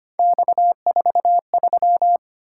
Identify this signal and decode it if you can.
Morse code, heard as a keyed tone, about 29 wpm, 700 Hz. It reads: X43
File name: Speed-Racing - Three Letter-Number Combinations - 25wpm